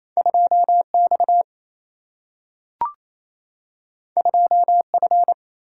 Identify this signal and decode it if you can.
Morse code, heard as a keyed tone, about 28 wpm, 700 Hz. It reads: 2X 2F
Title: Single-Letter Single-Number 28wpm